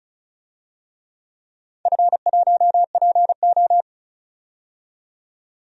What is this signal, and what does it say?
Morse code, heard as a keyed tone, about 36 wpm, 700 Hz. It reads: F1PO
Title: DXCC-100 Call Signs - 4 Characters - 35wpm